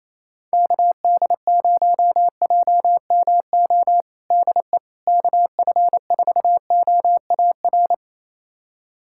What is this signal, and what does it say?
Morse code, heard as a keyed tone, about 28 wpm, 700 Hz. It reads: KD0JMO DE KF4OAR
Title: USA Callsign DE Callsign - 6 Characters - 28wpm